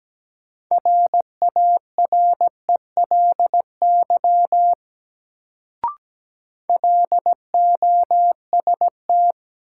Morse code, heard as a keyed tone, about 17 wpm, 700 Hz. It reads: RARELY LOST
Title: Sentences from Top 1400 Words - New Word Drill - 17wpm